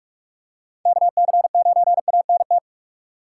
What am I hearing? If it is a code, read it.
Morse code, heard as a keyed tone, about 46 wpm, 690 Hz. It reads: KC9ANT
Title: USA Call Signs - 6 Characters - 2x Character Spacing - 45wpm